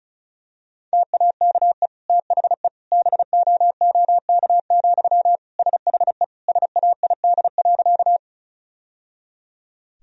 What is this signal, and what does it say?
Morse code, heard as a keyed tone, about 36 wpm, 700 Hz. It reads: TAKE THE BOOK, SHE SAID.
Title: Sentences from Top 1000 Words - Review Part 1 - 35wpm